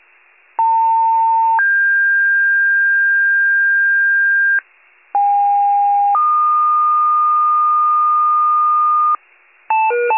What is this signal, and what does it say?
Two-tone page, heard as a one-second tone, then a three-second tone, about 900 then 1600 Hz, about 810 then 1220 Hz, with noise. It is heard from 0.6 to 9.2 s.